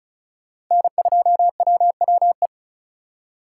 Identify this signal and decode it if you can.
Morse code, heard as a keyed tone, about 36 wpm, 700 Hz. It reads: N2WWE
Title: USA Call Signs - 2x Character Spacing - 35wpm